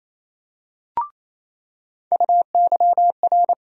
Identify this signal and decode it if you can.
Morse code, heard as a keyed tone, about 28 wpm, 700 Hz. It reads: UYR
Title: USA Call Sign Postfixes - 3 Letters 28wpm